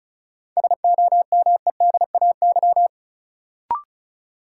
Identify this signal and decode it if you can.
Morse code, heard as a keyed tone, about 34 wpm, 700 Hz. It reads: SOMEDAY
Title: Sentences from Top 1600 Words - New Word Drill - 35wpm